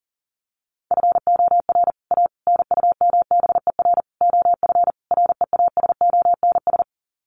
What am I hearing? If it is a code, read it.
Morse code, heard as a keyed tone, about 40 wpm, 700 Hz. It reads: FOR A NUMBER OF REASONS
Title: Sets of 5 Words Encore2 40wpm